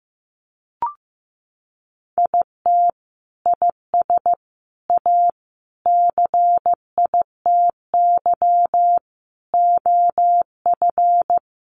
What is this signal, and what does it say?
Morse code, heard as a keyed tone, about 15 wpm, 700 Hz. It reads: IT IS A CITY OF
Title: Sentences from Top 1500 Words 15wpm